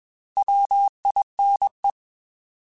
Morse code, heard as a keyed tone, about 21 wpm, 800 Hz. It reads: WINE
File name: Alphabetical / Alphabetical mp3s / Ws @20